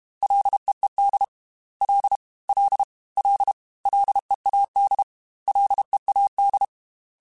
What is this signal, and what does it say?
Morse code, heard as a keyed tone, about 32 wpm, 800 Hz. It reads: LEED L L L LEAD LEAD
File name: Ls / Single L m3s / Single L @30